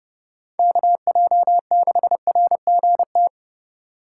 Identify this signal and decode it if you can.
Morse code, heard as a keyed tone, about 28 wpm, 700 Hz. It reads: KJ6RGT